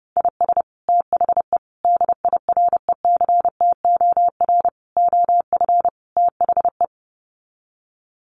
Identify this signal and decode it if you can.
Morse code, heard as a keyed tone, about 30 wpm, 700 Hz. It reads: IS THE DIRECTOR OF THE
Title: Sets of 5 Words Encore2 30wpm